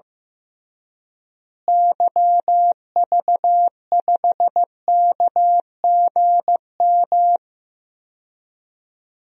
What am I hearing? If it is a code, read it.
Morse code, heard as a keyed tone, about 15 wpm, 700 Hz. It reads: YV5KGM
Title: DXCC-100 Call Signs - 6 Characters - Repeated 3x - 15wpm